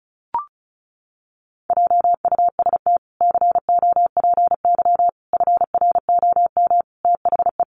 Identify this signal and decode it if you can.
Morse code, heard as a keyed tone, about 34 wpm, 700 Hz. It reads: JUST COPY FROM THE
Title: Sentences from Top 1400 Words 35wpm